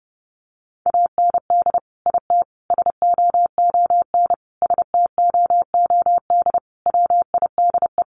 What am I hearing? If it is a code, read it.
Morse code, heard as a keyed tone, about 30 wpm, 700 Hz. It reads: AND IT SOON STOOD WIDE
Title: Sentences from Top 900 Words 30wpm